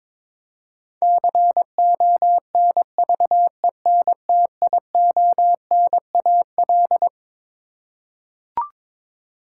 Morse code, heard as a keyed tone, about 22 wpm, 700 Hz. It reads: CONVENTIONAL
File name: Twelve Letter Words - 22wpm